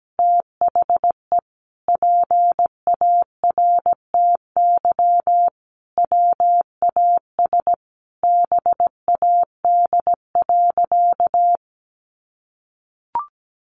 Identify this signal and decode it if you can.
Morse code, heard as a keyed tone, about 17 wpm, 700 Hz. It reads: THE PARTY WAS BAD.
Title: Sentences from Top 1500 Words 17wpm